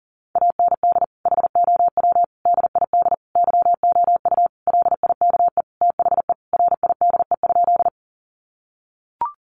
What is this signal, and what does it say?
Morse code, heard as a keyed tone, about 37 wpm, 700 Hz. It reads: AND HOW DID YOU LIKE THE RIDE?
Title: Sentences from Top 1200 Words 40wpm